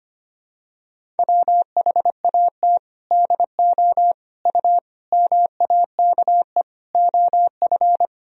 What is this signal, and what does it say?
Morse code, heard as a keyed tone, about 25 wpm, 700 Hz. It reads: WHAT DO U MAKE OF
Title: Sentences from Top 100 Words and Abbreviations 25wpm